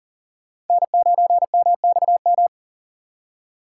Morse code, heard as a keyed tone, about 51 wpm, 700 Hz. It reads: N9MXM